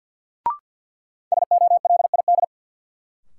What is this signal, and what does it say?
Morse code, heard as a keyed tone, about 53 wpm, 700 Hz. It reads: SOLID
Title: Sentences from Top 1500 Words - New Word Drill Alt - 50wpm